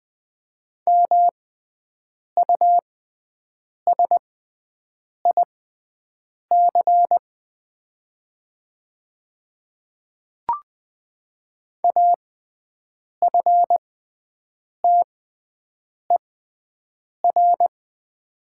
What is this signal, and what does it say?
Morse code, heard as a keyed tone, about 21 wpm, spaced as for 7 wpm, 700 Hz. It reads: MUSIC AFTER
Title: 6x Character Spacing - Top 300 Words - 20wpm